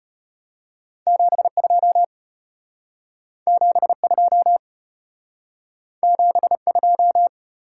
Morse code, heard as a keyed tone, about 33 wpm, 700 Hz. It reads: 7 2 7 2 7 2